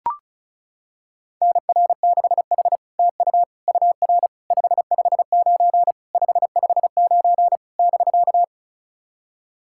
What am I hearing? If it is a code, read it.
Morse code, heard as a keyed tone, about 36 wpm, 700 Hz. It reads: NR6H TU UR 559 559 <BK>
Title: SOTA QSO Practice 35wpm